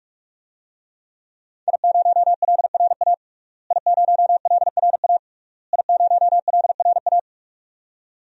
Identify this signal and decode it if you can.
Morse code, heard as a keyed tone, about 46 wpm, 700 Hz. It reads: I0LRA I0LRA I0LRA
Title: DXCC-100 Call Signs - 5 Characters - Repeated 3x - 45wpm